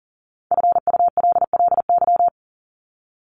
Morse code, heard as a keyed tone, about 38 wpm, 700 Hz. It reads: FULLY
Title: Top 1001-1250 Words 40wpm